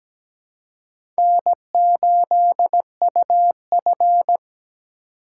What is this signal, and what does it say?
Morse code, heard as a keyed tone, about 17 wpm, 700 Hz. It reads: N8UF
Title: USA Call Signs - 3x Character Spacing - 17wpm